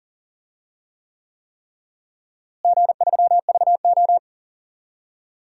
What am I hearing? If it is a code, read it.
Morse code, heard as a keyed tone, about 40 wpm, 700 Hz. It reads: G3VO
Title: DXCC-100 Call Signs - 4 Characters - Repeated 3x - 40wpm